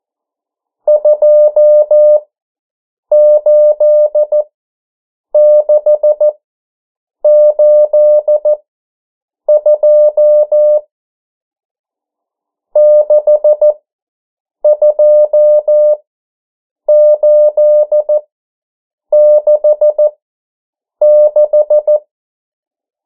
Morse code, heard as a keyed tone, about 14 wpm, spaced as for 7 wpm, 600 Hz. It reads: 28682 62866
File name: morse mp3 / 09 urok 8